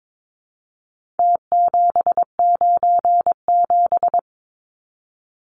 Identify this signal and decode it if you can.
Morse code, heard as a keyed tone, about 22 wpm, 700 Hz. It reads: T797
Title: Numbers - Contest Style - 4-Digit 22wpm